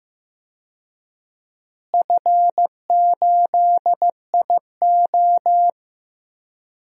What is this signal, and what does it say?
Morse code, heard as a keyed tone, about 15 wpm, 700 Hz. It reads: F8IO
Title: DXCC-100 Call Signs - 4 Characters - Repeated 3x - 15wpm